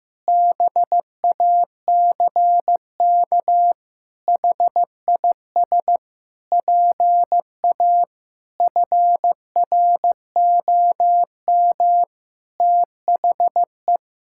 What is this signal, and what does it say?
Morse code, heard as a keyed tone, about 15 wpm, 700 Hz. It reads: BACK HIS PA FROM THE
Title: Pangram Challenge v2 - 15wpm